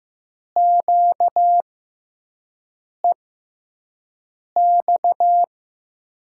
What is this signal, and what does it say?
Morse code, heard as a keyed tone, about 15 wpm, spaced as for 5 wpm, 700 Hz. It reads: QEX
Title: USA Call Signs - 6 Characters - 6x Character Spacing - 15wpm